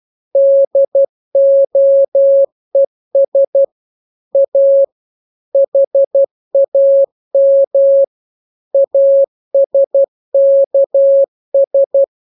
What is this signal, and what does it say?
Morse code, heard as a keyed tone, about 12 wpm, 550 Hz. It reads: DOES A HAM ASKS
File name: jokes / joke 16 12 0000